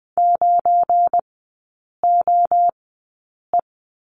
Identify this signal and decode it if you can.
Morse code, heard as a keyed tone, about 20 wpm, spaced as for 8 wpm, 700 Hz. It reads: 9OE
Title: 172 Lesson 28e - Number 9 - Three Characters 20wpm